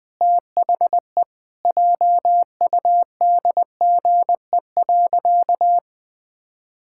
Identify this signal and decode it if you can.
Morse code, heard as a keyed tone, about 20 wpm, 700 Hz. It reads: THE JUDGE.
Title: Sentences with 3-Word Phrases - 20wpm